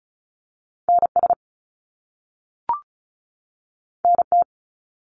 Morse code, heard as a keyed tone, about 36 wpm, 700 Hz. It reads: NS NT
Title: Canadian Provinces and Territories 35wpm